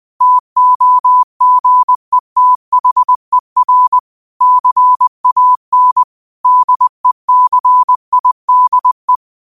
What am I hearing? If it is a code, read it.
Morse code, heard as a keyed tone, about 20 wpm, 1000 Hz. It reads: TOGETHER CAN DECIDE
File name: qotdfiles / 20wpm